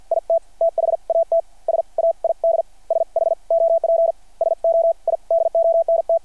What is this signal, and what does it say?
Morse code, heard as a keyed tone, about 50 wpm, spaced as for 42 wpm, 660 Hz. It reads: IT THAT SAID SHOP SO I DONT
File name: archived / CRSnet